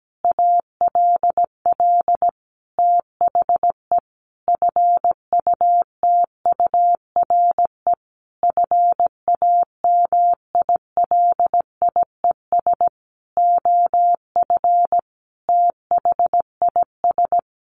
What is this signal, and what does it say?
Morse code, heard as a keyed tone, about 17 wpm, 700 Hz. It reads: ALL THE FUTURE FAMILIES OF THIS
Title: A Sound Of Thunder 17wpm